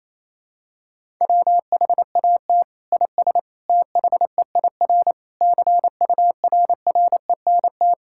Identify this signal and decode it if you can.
Morse code, heard as a keyed tone, about 28 wpm, 700 Hz. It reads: WHAT IS THEIR CURRENT
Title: Sentences from Top 1000 Words - Review Part 6 - 28wpm